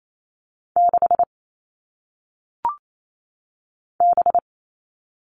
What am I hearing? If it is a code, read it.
Morse code, heard as a keyed tone, about 28 wpm, 700 Hz. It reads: 6 B
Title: All Cut Numbers - 1-Digit 28wpm